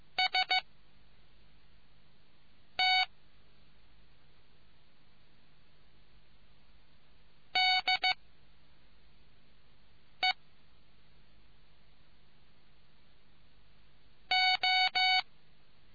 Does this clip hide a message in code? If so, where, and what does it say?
Yes: Morse code from 0.2 to 15.2 s, heard as a keyed tone, about 15 wpm, 750 Hz. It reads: S T D E O